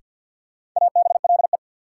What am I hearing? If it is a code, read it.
Morse code, heard as a keyed tone, about 52 wpm, 700 Hz. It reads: ABLE
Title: Sets of 2 Words - New Word Drill - 50wpm